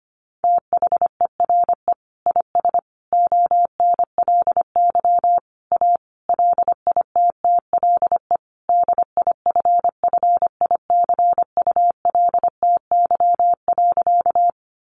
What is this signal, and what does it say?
Morse code, heard as a keyed tone, about 25 wpm, 700 Hz. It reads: THERE IS ONLY A LITTLE DIFFICULTY.
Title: Sentences from Top 1500 Words 25wpm